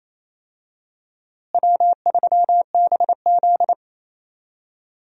Morse code, heard as a keyed tone, about 27 wpm, 700 Hz. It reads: W3BZ
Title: USA Call Signs - 4 Characters - Repeated 3x - 28wpm